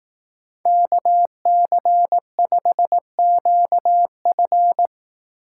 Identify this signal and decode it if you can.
Morse code, heard as a keyed tone, about 18 wpm, 700 Hz. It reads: KC5QF
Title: Speed Racing - USA 2x2 Call Signs 15wpm